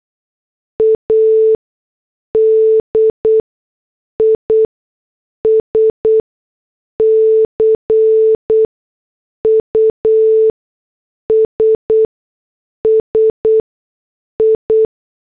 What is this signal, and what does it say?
Morse code, heard as a keyed tone, about 8 wpm, spaced as for 6 wpm, 440 Hz. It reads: A DISCUSSI
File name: morse / headlines